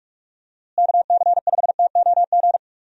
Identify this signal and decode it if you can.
Morse code, heard as a keyed tone, about 46 wpm, 700 Hz. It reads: KC5TOG
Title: USA Call Signs - 6 Characters - Repeated 2x - 45wpm